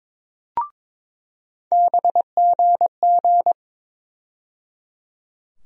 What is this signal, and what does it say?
Morse code, heard as a keyed tone, about 22 wpm, 700 Hz. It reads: BGG
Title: USA Call Sign Postfixes - 3 Letters 22wpm